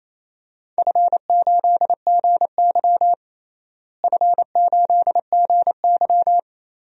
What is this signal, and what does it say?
Morse code, heard as a keyed tone, about 28 wpm, 700 Hz. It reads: F8GY F8GY
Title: DXCC-100 Call Signs - 4 Characters - Repeated 2x - 28wpm